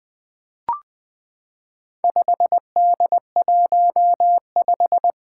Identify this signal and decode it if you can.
Morse code, heard as a keyed tone, about 20 wpm, 700 Hz. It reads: 5D15
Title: Four Challenging Letter-Number Combinations 20wpm